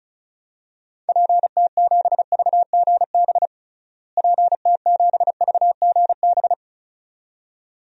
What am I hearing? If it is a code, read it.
Morse code, heard as a keyed tone, about 36 wpm, 700 Hz. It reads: PT7VGB PT7VGB
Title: DXCC-100 Call Signs - 6 Characters - Repeated 2x - 35wpm